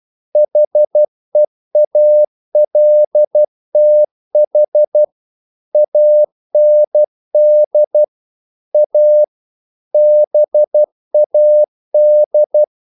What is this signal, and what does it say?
Morse code, heard as a keyed tone, about 12 wpm, 600 Hz. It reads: HEALTH AND A BAD
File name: CW / Practice / EnglishQuotations0312